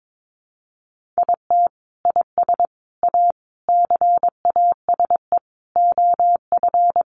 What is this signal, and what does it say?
Morse code, heard as a keyed tone, about 22 wpm, 700 Hz. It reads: IT IS A CASE OF
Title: Sentences from Top 1400 Words 22wpm